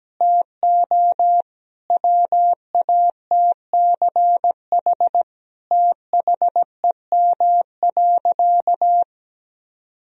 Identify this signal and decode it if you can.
Morse code, heard as a keyed tone, about 17 wpm, 700 Hz. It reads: TO WATCH THEM.
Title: Sentences from Top 1000 Words - Review Part 1 - 17wpm